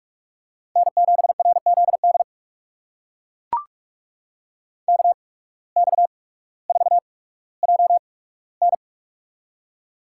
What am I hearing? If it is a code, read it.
Morse code, heard as a keyed tone, about 42 wpm, spaced as for 19 wpm, 700 Hz. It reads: N7RZD KX4JN
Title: USA Call Signs - 5 Characters - 8x Character Spacing - 45wpm